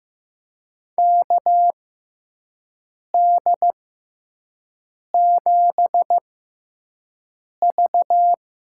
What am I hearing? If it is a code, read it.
Morse code, heard as a keyed tone, about 16 wpm, spaced as for 5 wpm, 700 Hz. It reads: KD7V
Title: USA Call Signs - 6 Characters - 6x Character Spacing - 15wpm